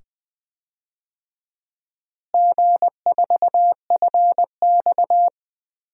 Morse code, heard as a keyed tone, about 20 wpm, 700 Hz. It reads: G4FX
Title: DXCC-100 Call Signs - 4 Characters - Repeated 2x - 20wpm